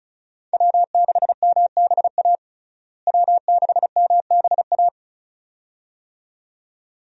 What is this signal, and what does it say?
Morse code, heard as a keyed tone, about 36 wpm, 700 Hz. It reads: W6MBA W6MBA
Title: USA Call Signs - 5 Characters - Repeated 2x - 35wpm